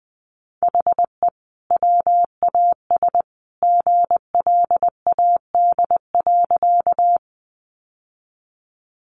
Morse code, heard as a keyed tone, about 20 wpm, 700 Hz. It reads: HE WAS GLAD.
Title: Sentences from Top 800 Words 20wpm